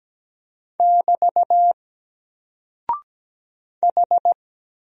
Morse code, heard as a keyed tone, about 17 wpm, 700 Hz. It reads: <BT> H
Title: Single Character (Letter:Number:Punctuation:Prosign) Alt 17wpm